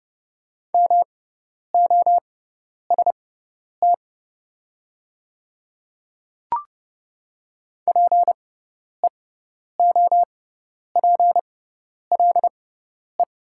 Morse code, heard as a keyed tone, about 29 wpm, spaced as for 10 wpm, 700 Hz. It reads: MOST PEOPLE
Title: 6x Character Spacing - Top 100 Words - 30wpm